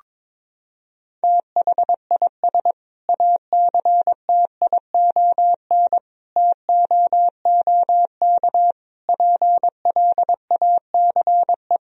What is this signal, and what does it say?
Morse code, heard as a keyed tone, about 22 wpm, 700 Hz. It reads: THIS ACTION TOOK PLACE
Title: Sentences from Top 1000 Words - Review Part 4 - 22wpm